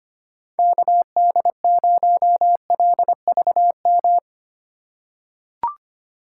Morse code, heard as a keyed tone, about 25 wpm, 700 Hz. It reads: KD0LVM
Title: USA Call Signs - 6 Characters - 2x Character Spacing - 25wpm